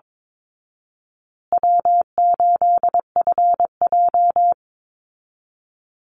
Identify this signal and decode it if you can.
Morse code, heard as a keyed tone, about 22 wpm, 700 Hz. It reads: W8FJ
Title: Easy Call Signs Repeated 2x - 22wpm